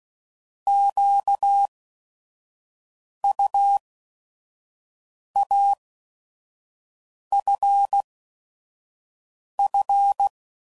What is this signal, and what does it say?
Morse code, heard as a keyed tone, about 16 wpm, spaced as for 5 wpm, 800 Hz. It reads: QUAFF
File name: Prefixes / By First Letter / Qs